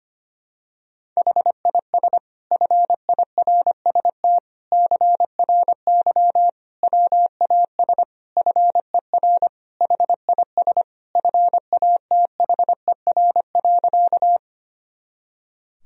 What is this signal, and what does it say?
Morse code, heard as a keyed tone, about 25 wpm, 700 Hz. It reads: HIS FIRST CRY WAS FER HIS FATHER.
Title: Sentences from Top 500 Words and Abbreviations 25wpm